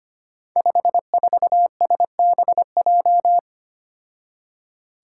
Morse code, heard as a keyed tone, about 25 wpm, 700 Hz. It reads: 54SBJ
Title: Five Challenging Letter-Number Combinations 25wpm